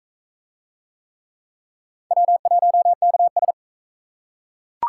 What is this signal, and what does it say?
Morse code, heard as a keyed tone, about 40 wpm, 700 Hz. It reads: W1KS